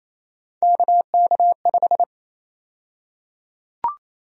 Morse code, heard as a keyed tone, about 27 wpm, 700 Hz. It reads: KK5
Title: USA Call Sign Prefixes - 28wpm